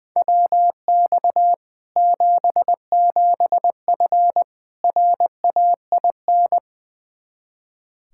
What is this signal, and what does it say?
Morse code, heard as a keyed tone, about 20 wpm, 700 Hz. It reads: WX 77F RAIN
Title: Practice QSO - Exchange 2 - Info - 20wpm